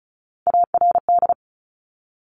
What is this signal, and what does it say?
Morse code, heard as a keyed tone, about 36 wpm, 700 Hz. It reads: ARD
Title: Top Three Letter Combinations in Words 35wpm